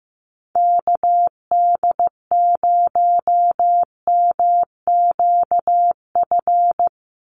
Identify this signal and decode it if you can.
Morse code, heard as a keyed tone, about 15 wpm, 700 Hz. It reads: KD0MQF
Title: USA Call Signs 15wpm